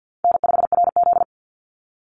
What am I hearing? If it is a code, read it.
Morse code, heard as a keyed tone, about 50 wpm, 700 Hz. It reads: N5RZ
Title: Easy Call Signs - 50wpm